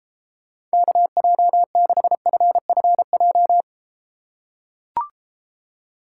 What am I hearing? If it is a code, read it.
Morse code, heard as a keyed tone, about 33 wpm, 700 Hz. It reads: KJ6FFJ